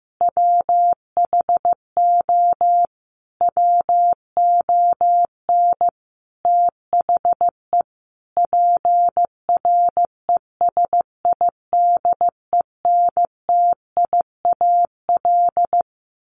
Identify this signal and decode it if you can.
Morse code, heard as a keyed tone, about 15 wpm, 700 Hz. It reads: WHO WON THE PRESIDENTIAL
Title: A Sound Of Thunder 15wpm